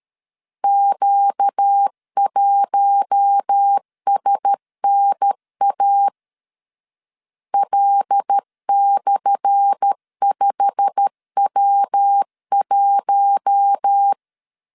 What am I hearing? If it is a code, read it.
Morse code, heard as a keyed tone, about 13 wpm, 800 Hz. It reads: Q1SNA L/5W1
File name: LEZIONI / 10 SIGNS / 10A SLASH 12wpm 2314